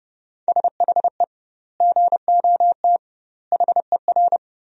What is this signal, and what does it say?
Morse code, heard as a keyed tone, about 32 wpm, 700 Hz. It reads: SHE GOT HER
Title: 2x Word-Spacing - Sentences from Top 500 Words - 30wpm